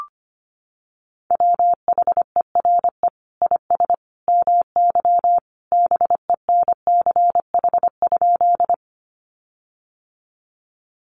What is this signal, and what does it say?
Morse code, heard as a keyed tone, about 25 wpm, 700 Hz. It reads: WHERE IS MY BENCH?
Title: Sentences from Top 1400 Words 25wpm